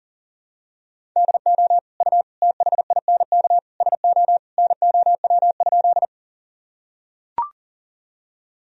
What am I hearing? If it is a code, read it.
Morse code, heard as a keyed tone, about 40 wpm, 700 Hz. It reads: DO U THINK SO NOW?
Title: Sentences from Top 200 Words and Abbreviations 40wpm